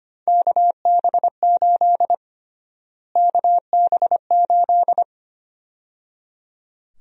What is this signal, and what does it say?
Morse code, heard as a keyed tone, about 25 wpm, 700 Hz. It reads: KB8 KB8
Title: USA Call Sign Prefixes - Repeated 2x - 25wpm